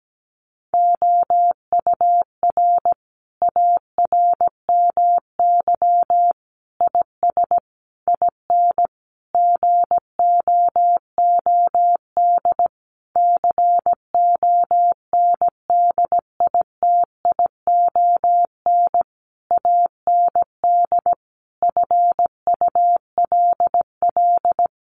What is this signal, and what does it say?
Morse code, heard as a keyed tone, about 17 wpm, 700 Hz. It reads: OUR ARMY IS IN GOOD CONDITION AND FULL
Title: Sentences from Top 1500 Words 17wpm